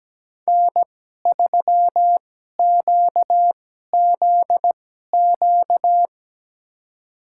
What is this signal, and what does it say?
Morse code, heard as a keyed tone, about 17 wpm, spaced as for 12 wpm, 700 Hz. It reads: N3QZQ
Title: USA Call Signs - 5 Characters - 2x Character Spacing - 17wpm